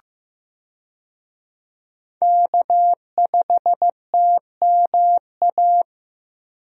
Morse code, heard as a keyed tone, about 15 wpm, 700 Hz. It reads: K5TMA
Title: USA Call Signs - 5 Characters - 15wpm